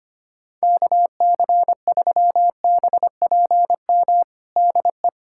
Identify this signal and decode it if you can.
Morse code, heard as a keyed tone, about 25 wpm, 700 Hz. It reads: KC3BPM DE
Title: USA Callsign DE Callsign - 6 Characters - 25wpm